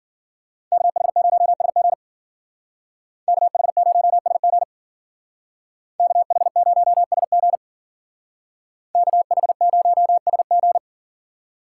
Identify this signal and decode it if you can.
Morse code, heard as a keyed tone, about 51 wpm, 700 Hz. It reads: KH0SG KH0SG KH0SG KH0SG